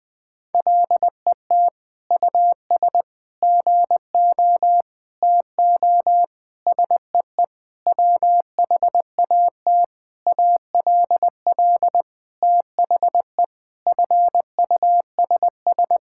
Morse code, heard as a keyed tone, about 21 wpm, 700 Hz. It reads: LET US GO TO SEE WHAT ALL THE FUSS
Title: Sentences with 8-Word Phrases - 20wpm